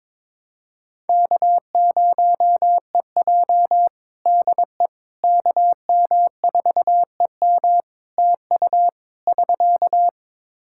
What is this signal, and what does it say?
Morse code, heard as a keyed tone, about 22 wpm, 700 Hz. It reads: K0EJ DE KM4EM TU <SK>
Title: Practice QSO - Exchange 4 - Ending - 22wpm